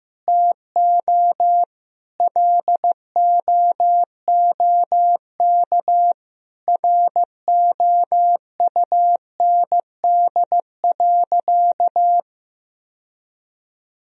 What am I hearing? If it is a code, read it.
Morse code, heard as a keyed tone, about 15 wpm, 700 Hz. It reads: TO LOOK ROUND.
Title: Sentences from Top 1000 Words - Review Part 1 - 15wpm